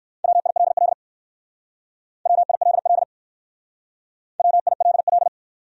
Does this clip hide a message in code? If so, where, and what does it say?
Yes: Morse code from 0.2 to 5.3 s, heard as a keyed tone, about 63 wpm, 700 Hz. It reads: WILL WILL WILL